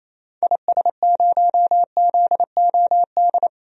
Morse code, heard as a keyed tone, about 28 wpm, 700 Hz. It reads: IS0ZOD
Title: DXCC-100 Call Signs - 6 Characters - 28wpm